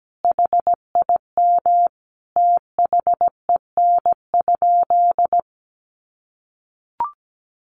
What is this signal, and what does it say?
Morse code, heard as a keyed tone, about 17 wpm, 700 Hz. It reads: HIM THEN?
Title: Sentences from Top 1300 Words 17wpm